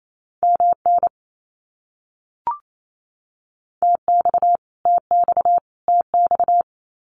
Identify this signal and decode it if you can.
Morse code, heard as a keyed tone, about 28 wpm, 700 Hz. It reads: MN TX TX TX
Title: US States Repeated 3x 28wpm